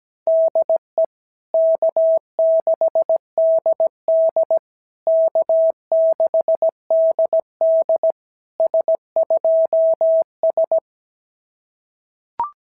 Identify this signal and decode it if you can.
Morse code, heard as a keyed tone, about 17 wpm, 650 Hz. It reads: DE K6DD K6DD S2S
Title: SOTA QSO Practice 17wpm